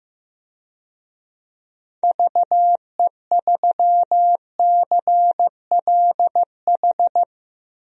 Morse code, heard as a keyed tone, about 15 wpm, 700 Hz. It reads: VE3CLH
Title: DXCC-100 Call Signs - 6 Characters - 15wpm